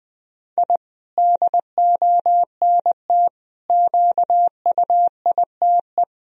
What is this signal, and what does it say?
Morse code, heard as a keyed tone, about 20 wpm, 700 Hz. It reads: I DONT QUITE
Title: Sentences from Top 1000 Words - Review Part 4 - 20wpm